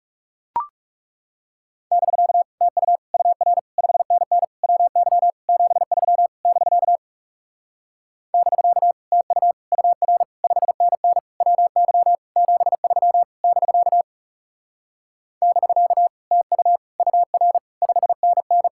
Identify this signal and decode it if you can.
Morse code, heard as a keyed tone, about 40 wpm, 700 Hz. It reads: <BK> TU UR 5NN WY 73 <BK> <BK> TU UR 5NN WY 73 <BK> <BK> TU UR 5NN